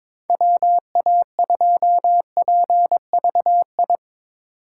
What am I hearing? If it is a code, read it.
Morse code, heard as a keyed tone, about 22 wpm, 700 Hz. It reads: WA2PVI
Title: USA Call Signs - 6 Characters - 3x Character Spacing - 22wpm